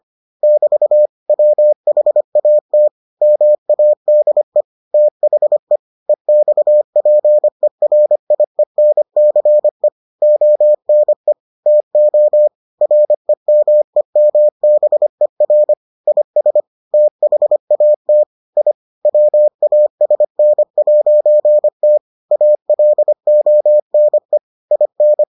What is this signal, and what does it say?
Morse code, heard as a keyed tone, about 25 wpm, 600 Hz. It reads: <BT> WHAT MADE THE EXPERIENCE ONE TO REMEMBER IS THAT I WASN'T ALONE IN